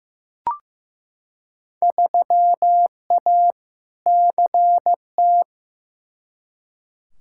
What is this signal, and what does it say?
Morse code, heard as a keyed tone, about 15 wpm, 700 Hz. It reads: 3A CT
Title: Field Day Exchanges - 15wpm